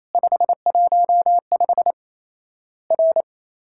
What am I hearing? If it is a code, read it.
Morse code, heard as a keyed tone, about 28 wpm, 700 Hz. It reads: 515 R
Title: Worked All Europe DX Contest - QTC Exchange - 28wpm